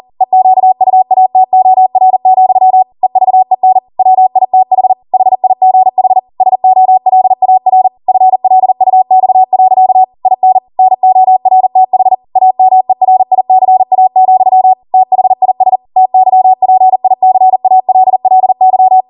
Morse code, heard as a keyed tone, about 40 wpm, 750 Hz. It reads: EQUATOR, EVEN WITH HIGH SOLAR FLUX. IN NORTH AMERICA, THIS TYPICALLY